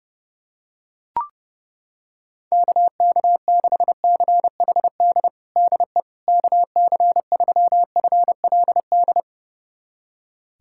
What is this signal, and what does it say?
Morse code, heard as a keyed tone, about 32 wpm, 700 Hz. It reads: KK6CHD DE KC3FLD
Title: USA Callsign DE Callsign - 6 Characters - 30wpm